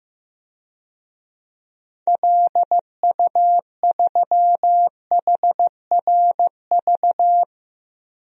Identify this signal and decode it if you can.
Morse code, heard as a keyed tone, about 16 wpm, 700 Hz. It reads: LU3HRV
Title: DXCC-100 Call Signs - 6 Characters - Repeated 3x - 15wpm